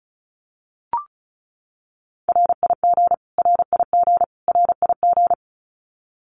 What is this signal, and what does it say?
Morse code, heard as a keyed tone, about 34 wpm, 700 Hz. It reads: RIG RIG RIG
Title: Abbreviations Repeated 3x - 35wpm